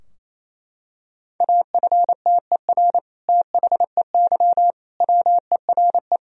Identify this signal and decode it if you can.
Morse code, heard as a keyed tone, about 28 wpm, 700 Hz. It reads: AFTER THEY WERE
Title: Sentences from Top 1000 Words - Review Part 2 - 28wpm